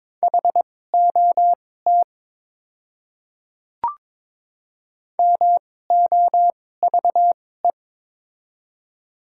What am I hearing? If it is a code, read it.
Morse code, heard as a keyed tone, about 22 wpm, spaced as for 16 wpm, 700 Hz. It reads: HOT MOVE
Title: 2x Character Spacing - Top 500 Words - 22wpm